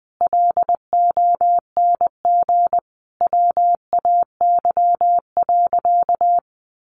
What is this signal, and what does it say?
Morse code, heard as a keyed tone, about 20 wpm, 700 Hz. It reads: LONG WAY.
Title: Sentences from Top 600 Words 20wpm